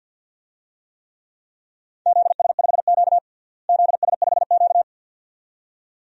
Morse code, heard as a keyed tone, about 51 wpm, 700 Hz. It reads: ZS5Q ZS5Q
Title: DXCC-100 Call Signs - 4 Characters - Repeated 2x - 50wpm